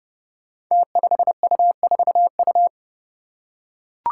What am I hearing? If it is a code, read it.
Morse code, heard as a keyed tone, about 32 wpm, 700 Hz. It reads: T5U4U
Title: Five Challenging Letter-Number Combinations 30wpm